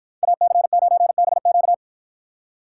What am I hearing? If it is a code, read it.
Morse code, heard as a keyed tone, about 51 wpm, 700 Hz. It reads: AC9BQ